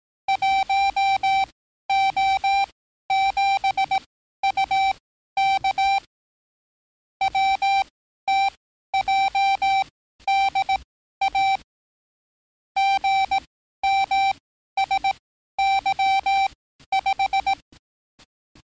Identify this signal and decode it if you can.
Morse code, heard as a keyed tone, about 18 wpm, spaced as for 12 wpm, 780 Hz. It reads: 1O7UK WTJDA GMSY5
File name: training / NAVPERS 11049to11060 / NAVPERS 11054 RA RB